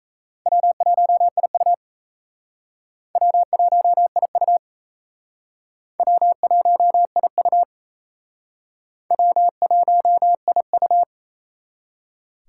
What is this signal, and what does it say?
Morse code, heard as a keyed tone, about 38 wpm, 700 Hz. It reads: W1IU W1IU W1IU W1IU